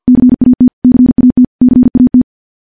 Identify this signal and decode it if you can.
Morse code, heard as a keyed tone, about 50 wpm, 250 Hz. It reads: CAT CAT CAT
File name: morse / morse-training-50WPM